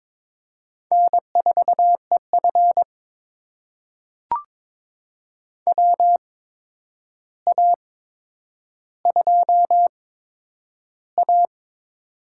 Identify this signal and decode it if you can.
Morse code, heard as a keyed tone, about 22 wpm, spaced as for 9 wpm, 700 Hz. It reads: N4EF WA2A
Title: USA Call Signs - 4 Characters - 8x Character Spacing - 22wpm